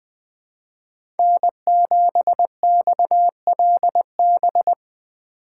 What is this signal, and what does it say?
Morse code, heard as a keyed tone, about 21 wpm, 700 Hz. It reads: N7XLB
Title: USA Call Signs - 5 Characters - 4x Character Spacing - 20wpm